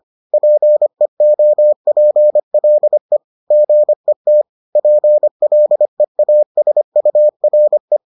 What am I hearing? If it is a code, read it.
Morse code, heard as a keyed tone, about 25 wpm, 600 Hz. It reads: PEOPLE GET PLEASURE